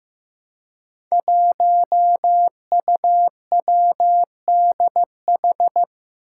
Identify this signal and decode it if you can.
Morse code, heard as a keyed tone, about 15 wpm, 700 Hz. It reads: 1UWDH
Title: Five Challenging Letter-Number Combinations 15wpm